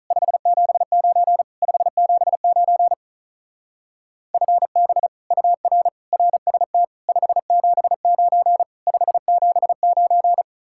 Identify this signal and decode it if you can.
Morse code, heard as a keyed tone, about 42 wpm, 700 Hz. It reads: 579 579 FB UR RST 579 579
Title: Speed-Racing Common QSO Phrases Encore - 35wpm